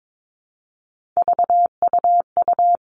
Morse code, heard as a keyed tone, about 22 wpm, 700 Hz. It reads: VUU
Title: All Cut Numbers - 3-Digit 22wpm